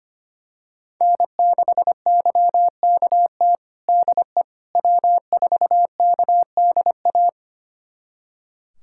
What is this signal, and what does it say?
Morse code, heard as a keyed tone, about 25 wpm, 700 Hz. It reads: N6YKT DE W4KDA
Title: USA Callsign DE Callsign - 5 Characters - 25wpm